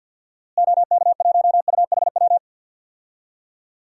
Morse code, heard as a keyed tone, about 46 wpm, 700 Hz. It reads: OK1UHW
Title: DXCC-100 Call Signs - 6 Characters - Repeated 3x - 50wpm